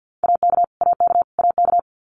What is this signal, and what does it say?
Morse code, heard as a keyed tone, about 49 wpm, 700 Hz. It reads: AK AK AK
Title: US States Repeated 3x 50wpm